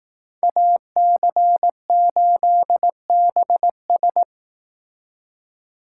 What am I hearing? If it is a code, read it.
Morse code, heard as a keyed tone, about 18 wpm, 700 Hz. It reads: AC8BS
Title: Speed Racing - USA Call Signs 15wpm